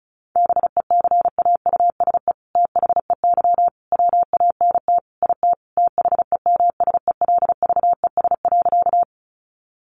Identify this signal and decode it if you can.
Morse code, heard as a keyed tone, about 34 wpm, 700 Hz. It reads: BECAUSE THEY WANT IT THEMSELVES.
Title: Sentences from Top 800 Words 35wpm